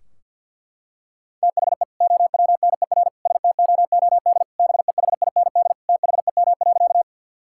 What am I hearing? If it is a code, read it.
Morse code, heard as a keyed tone, about 50 wpm, 700 Hz. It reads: THE OWNER STOOD BEHIND THEM.
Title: Sentences from Top 1600 Words 50wpm